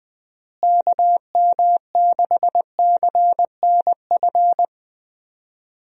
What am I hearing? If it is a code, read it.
Morse code, heard as a keyed tone, about 20 wpm, 700 Hz. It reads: KM6CNF
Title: USA Call Signs - 6 Characters - Repeated 2x - 20wpm